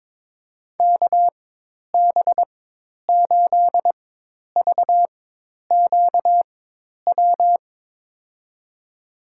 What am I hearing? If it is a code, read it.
Morse code, heard as a keyed tone, about 22 wpm, spaced as for 10 wpm, 700 Hz. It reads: KB8VQW
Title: USA Call Signs - 4x Character Spacing - 22wpm